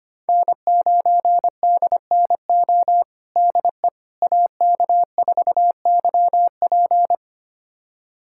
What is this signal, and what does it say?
Morse code, heard as a keyed tone, about 25 wpm, 700 Hz. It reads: N9DNO DE AK4YP
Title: USA Callsign DE Callsign - 5 Characters - 25wpm